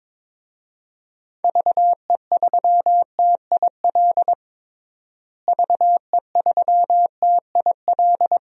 Morse code, heard as a keyed tone, about 22 wpm, 700 Hz. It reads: VE3TIL VE3TIL
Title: DXCC-100 Call Signs - 6 Characters - Repeated 3x - 22wpm